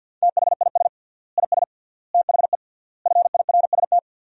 Morse code, heard as a keyed tone, about 48 wpm, 700 Hz. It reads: THIS IS THE FIRST
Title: 3x Word-Spacing - Sentences from Top 200 Words - 50wpm